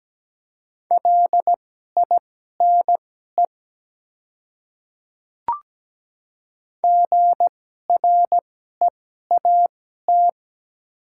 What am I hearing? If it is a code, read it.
Morse code, heard as a keyed tone, about 17 wpm, spaced as for 12 wpm, 700 Hz. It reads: LINE GREAT
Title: 2x Character Spacing - Top 300 Words - 17wpm